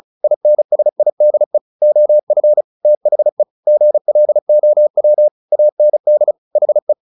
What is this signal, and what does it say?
Morse code, heard as a keyed tone, about 34 wpm, 600 Hz. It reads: INSIDE OF THE GLOW AND HE
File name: the old man and the sea 2 35wpm